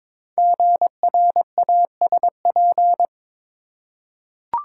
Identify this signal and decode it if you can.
Morse code, heard as a keyed tone, about 22 wpm, 700 Hz. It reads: GRASP
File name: 5 Letter Words - First 3 Letters Given - 22wpm